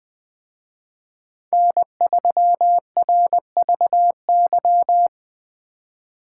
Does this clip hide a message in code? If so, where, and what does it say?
Yes: Morse code from 1.5 to 5.1 s, heard as a keyed tone, about 20 wpm, 700 Hz. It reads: N3RVY